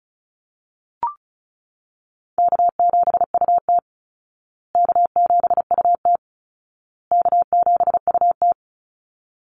Morse code, heard as a keyed tone, about 34 wpm, 700 Hz. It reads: K7UT K7UT K7UT
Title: Easy Call Signs Repeated 3x - 35wpm